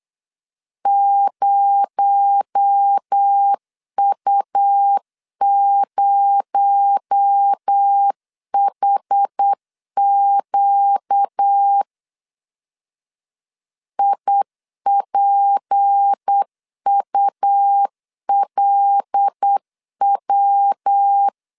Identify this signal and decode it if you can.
Morse code, heard as a keyed tone, about 8 wpm, 800 Hz. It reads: 0U0HQ IPULW
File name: LEZIONI / 6 PKX0 / 6 PKX0 8wpm 3341